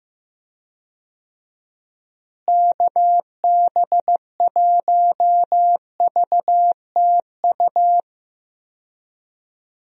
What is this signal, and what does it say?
Morse code, heard as a keyed tone, about 15 wpm, 700 Hz. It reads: KB1VTU
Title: USA Call Signs - 6 Characters - Repeated 3x - 15wpm